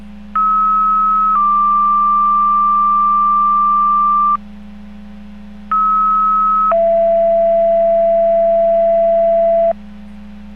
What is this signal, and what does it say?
Two-tone page, heard as a one-second tone, then a three-second tone, about 1290 then 1180 Hz, about 1290 then 670 Hz. It is heard from 0.3 to 9.7 s.